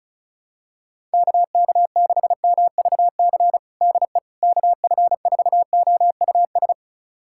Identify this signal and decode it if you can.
Morse code, heard as a keyed tone, about 36 wpm, 700 Hz. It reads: KK6MVC DE KF4OUS
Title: USA Callsign DE Callsign - 6 Characters - 35wpm